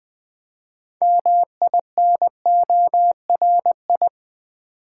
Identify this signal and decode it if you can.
Morse code, heard as a keyed tone, about 20 wpm, 700 Hz. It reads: MINORI
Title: Build A Word - 3 Letters At A Time - 8-Letter Words - 20wpm